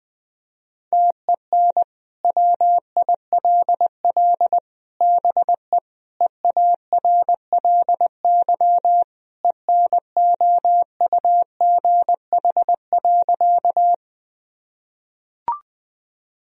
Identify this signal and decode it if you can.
Morse code, heard as a keyed tone, about 21 wpm, 700 Hz. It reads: TEN WILL BE EARLY ENOUGH.
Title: Sentences from Top 1000 Words - Review Part 2 - 20wpm